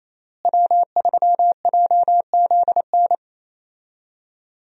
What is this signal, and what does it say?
Morse code, heard as a keyed tone, about 28 wpm, 700 Hz. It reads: W3JZN
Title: USA Call Signs - 5 Characters - Repeated 2x - 28wpm